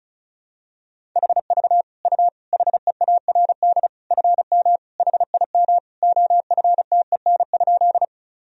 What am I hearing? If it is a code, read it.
Morse code, heard as a keyed tone, about 34 wpm, 700 Hz. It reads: HV U HEARD FM HIM OFTEN?
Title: Sentences from Top 400 Words and Abbreviations 35wpm